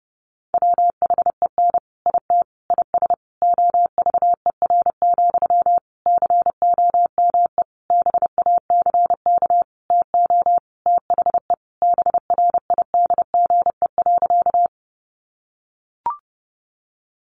Morse code, heard as a keyed tone, about 28 wpm, 700 Hz. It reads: WHEN IT IS OVER, COME BACK TO THE BRIDGE.
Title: Sentences from Top 1200 Words 30wpm